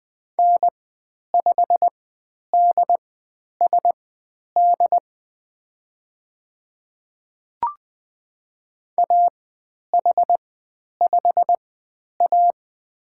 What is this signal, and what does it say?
Morse code, heard as a keyed tone, about 20 wpm, spaced as for 10 wpm, 700 Hz. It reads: N5DSD AH5A
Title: 095 Lesson 16g - Letter c - Call Signs 20wpm